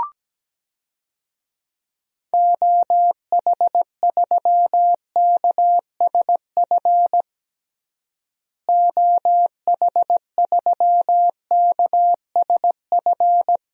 Morse code, heard as a keyed tone, about 17 wpm, 700 Hz. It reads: OH3KSF OH3KSF
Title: DXCC-100 Call Signs - 6 Characters - Repeated 2x - 17wpm